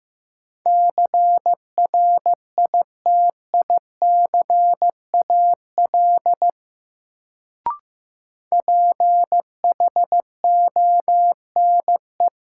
Morse code, heard as a keyed tone, about 15 wpm, 700 Hz. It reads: CRITICAL PHONE
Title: Sentences from Top 1100 Words - New Word Drill - 15wpm